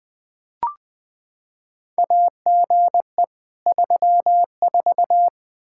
Morse code, heard as a keyed tone, about 20 wpm, 700 Hz. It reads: AGE 34
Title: K1USN SST QSO Practice 20wpm